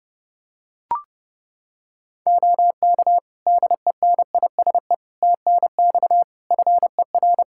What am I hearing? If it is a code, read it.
Morse code, heard as a keyed tone, about 30 wpm, 700 Hz. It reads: OK DENISE TNX FER
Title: K1USN SST QSO Practice 30wpm